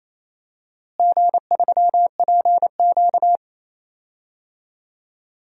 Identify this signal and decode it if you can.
Morse code, heard as a keyed tone, about 28 wpm, 700 Hz. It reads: G3PQ
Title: DXCC-100 Call Signs - 4 Characters - Repeated 3x - 28wpm